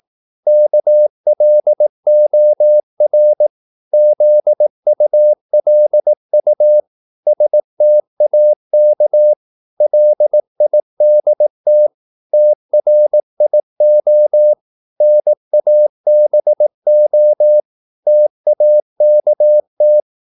Morse code, heard as a keyed tone, about 18 wpm, 600 Hz. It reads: KLOR ZULU STAK LIDT TRIO NABO TAKT
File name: Korte ord DK 18wpm